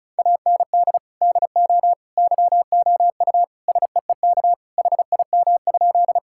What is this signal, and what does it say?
Morse code, heard as a keyed tone, about 36 wpm, 700 Hz. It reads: AND DO YOU SEEK HIM?
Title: Sentences from Top 1600 Words 35wpm